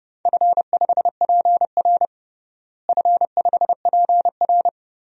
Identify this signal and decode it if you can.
Morse code, heard as a keyed tone, about 30 wpm, 700 Hz. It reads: F5PR F5PR
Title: DXCC-100 Call Signs - 4 Characters - Repeated 3x - 30wpm